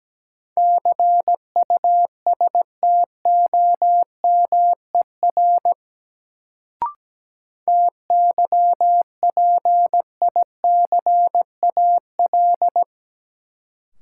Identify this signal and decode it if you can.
Morse code, heard as a keyed tone, about 17 wpm, 700 Hz. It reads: CUSTOMER TYPICAL
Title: Sentences from Top 1100 Words - New Word Drill - 17wpm